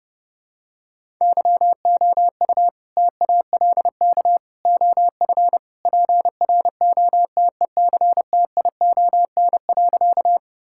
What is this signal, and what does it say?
Morse code, heard as a keyed tone, about 30 wpm, 700 Hz. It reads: YOU TALK OF PROTECTION.
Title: Sentences from Top 1600 Words 30wpm